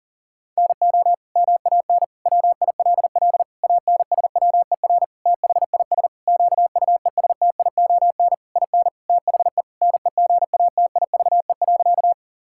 Morse code, heard as a keyed tone, about 40 wpm, 700 Hz. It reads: NO MAN WILL ANSWER THIS QUESTION IN THE NEGATIVE.
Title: Sentences from Top 1000 Words - Review Part 6 - 40wpm